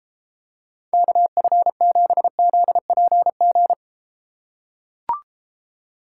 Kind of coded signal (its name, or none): Morse code